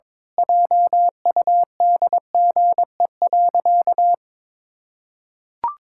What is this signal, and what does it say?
Morse code, heard as a keyed tone, about 22 wpm, 700 Hz. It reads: JUDGE.
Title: Pangram Challenge v2 - 22wpm